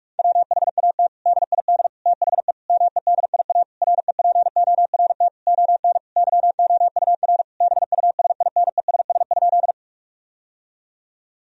Morse code, heard as a keyed tone, about 46 wpm, 700 Hz. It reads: WHAT DID THE MEDIA REPORT ON YOUR BUSINESS?
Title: Sentences from Top 1000 Words - Review Part 4 - 45wpm